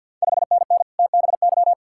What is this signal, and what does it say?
Morse code, heard as a keyed tone, about 48 wpm, 690 Hz. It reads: 5NN TBY
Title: Ontario QSO Party Exchange 50wpm